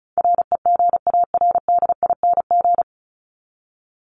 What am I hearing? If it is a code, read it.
Morse code, heard as a keyed tone, about 34 wpm, 700 Hz. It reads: REGARDING
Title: ING Suffix - Top 100 Words with Base Words 35wpm